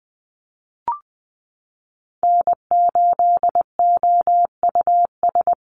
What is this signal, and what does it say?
Morse code, heard as a keyed tone, about 20 wpm, 700 Hz. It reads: N8OUS
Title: USA Call Signs Repeated 3x 20wpm